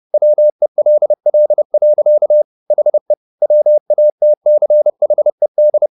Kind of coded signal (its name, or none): Morse code